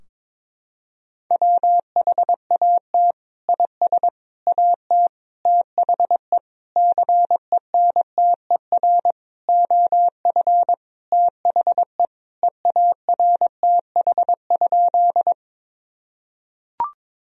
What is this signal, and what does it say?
Morse code, heard as a keyed tone, about 22 wpm, 700 Hz. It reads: WHAT IS AT THE CENTER OF THE EARTH?
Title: Sentences from Top 1000 Words - Review Part 2 - 22wpm